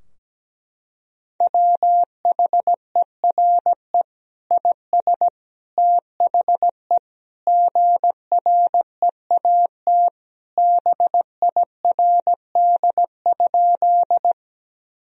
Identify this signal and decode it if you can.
Morse code, heard as a keyed tone, about 17 wpm, 700 Hz. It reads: WHERE IS THE GREAT BIRD?
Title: Sentences from Top 1000 Words - Review Part 2 - 17wpm